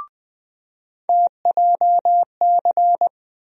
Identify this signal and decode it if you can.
Morse code, heard as a keyed tone, about 20 wpm, 700 Hz. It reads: TJC
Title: USA Call Sign Postfixes - 3 Letters 20wpm